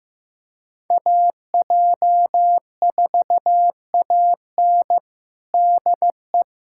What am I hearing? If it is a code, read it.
Morse code, heard as a keyed tone, about 16 wpm, 700 Hz. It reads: AJ4AN DE
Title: USA Callsign DE Callsign - 5 Characters - 15wpm